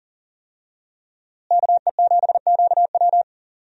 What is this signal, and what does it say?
Morse code, heard as a keyed tone, about 40 wpm, 700 Hz. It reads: KE7QW
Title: USA Call Signs - 5 Characters - Repeated 2x - 40wpm